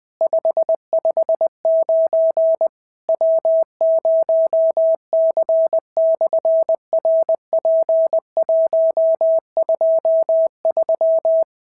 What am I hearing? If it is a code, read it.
Morse code, heard as a keyed tone, about 20 wpm, 650 Hz. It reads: 559 W0C/RP123
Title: SOTA QSO Practice 20wpm